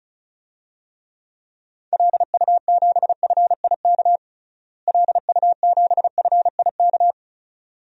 Morse code, heard as a keyed tone, about 36 wpm, 700 Hz. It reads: LU7FIK LU7FIK
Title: DXCC-100 Call Signs - 6 Characters - Repeated 2x - 35wpm